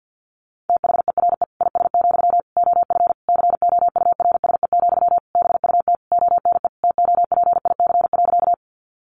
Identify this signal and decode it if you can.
Morse code, heard as a keyed tone, about 50 wpm, 700 Hz. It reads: THERE IS, OF COURSE, BUT ONE TOPIC.
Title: Sentences from Top 1200 Words 50wpm